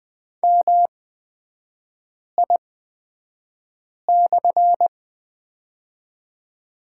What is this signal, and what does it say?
Morse code, heard as a keyed tone, about 20 wpm, spaced as for 5 wpm, 700 Hz. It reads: MI/
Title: 236 Lesson 38e - Letter q - Three Characters 20wpm